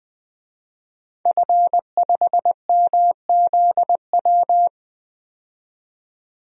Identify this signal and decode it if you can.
Morse code, heard as a keyed tone, about 20 wpm, 700 Hz. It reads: F5MZW
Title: DXCC-100 Call Signs - 5 Characters - 20wpm